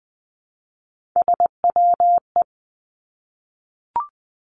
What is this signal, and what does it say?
Morse code, heard as a keyed tone, about 20 wpm, 700 Hz. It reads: SWE
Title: Three Challenging Letter-Number Combinations 20wpm